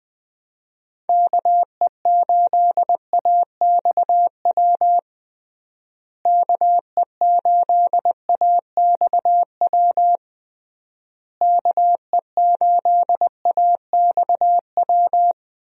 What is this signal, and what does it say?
Morse code, heard as a keyed tone, about 21 wpm, 700 Hz. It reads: KE8AXW KE8AXW KE8AXW
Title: USA Call Signs - 6 Characters - Repeated 3x - 20wpm